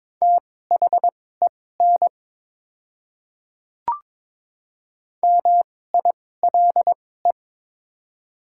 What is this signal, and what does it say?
Morse code, heard as a keyed tone, about 22 wpm, spaced as for 16 wpm, 690 Hz. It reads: THEN MILE
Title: 2x Character Spacing - Top 500 Words - 22wpm